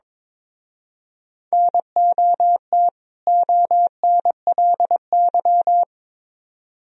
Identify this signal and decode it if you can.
Morse code, heard as a keyed tone, about 26 wpm, 700 Hz. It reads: NOT ONLY
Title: Speed-Racing - Sets of 2 Words 22wpm